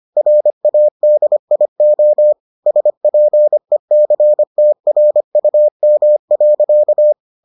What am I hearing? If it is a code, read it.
Morse code, heard as a keyed tone, about 25 wpm, 600 Hz. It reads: RADIO SPECTRUM.